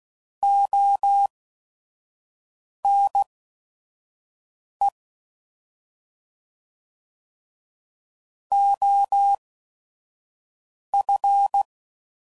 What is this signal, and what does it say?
Morse code, heard as a keyed tone, about 16 wpm, spaced as for 5 wpm, 800 Hz. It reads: ONE OF
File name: Prefixes / By First Letter / Os